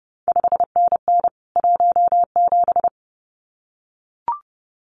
Morse code, heard as a keyed tone, about 32 wpm, 700 Hz. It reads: 5NN 17
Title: CQ WW DX Contest Exchange Encore - 30wpm